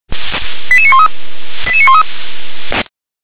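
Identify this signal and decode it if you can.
Five-tone selective call (ZVEI), heard as quick runs of five tones, tones of about 70 ms. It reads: 88913 88913
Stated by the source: ZVEI five-tone selective call received on an SDR.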